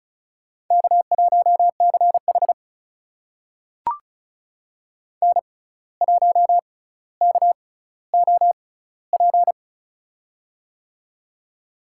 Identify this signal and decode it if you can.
Morse code, heard as a keyed tone, about 36 wpm, spaced as for 17 wpm, 700 Hz. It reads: K1CH N1KOP
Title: USA Call Signs - 6x Character Spacing - 35wpm